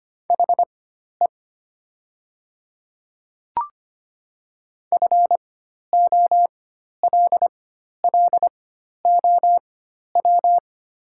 Morse code, heard as a keyed tone, about 25 wpm, spaced as for 12 wpm, 700 Hz. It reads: HE FOLLOW
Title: 4x Character Spacing - Top 200 Words - 25wpm